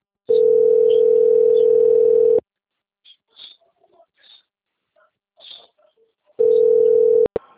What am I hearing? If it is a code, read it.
Ringback tone (North America), 440 + 480 Hz.